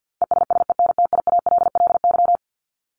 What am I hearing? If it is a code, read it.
Morse code, heard as a keyed tone, about 50 wpm, 700 Hz. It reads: ESSENTIALLY
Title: LY Suffix - Top 100 Words with Base Words 50wpm